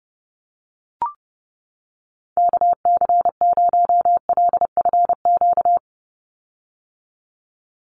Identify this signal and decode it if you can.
Morse code, heard as a keyed tone, about 30 wpm, 700 Hz. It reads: KC0LFQ
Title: USA Call Signs 30wpm